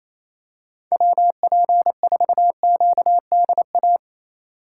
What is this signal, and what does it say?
Morse code, heard as a keyed tone, about 27 wpm, 700 Hz. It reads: WP4QDA
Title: USA Call Signs - 8x Character Spacing - 28wpm